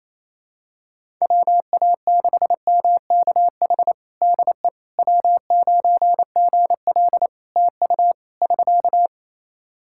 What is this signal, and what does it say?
Morse code, heard as a keyed tone, about 28 wpm, 700 Hz. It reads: WA6MKH DE W9GL TU <SK>
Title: Practice QSO - Exchange 4 - Ending - 28wpm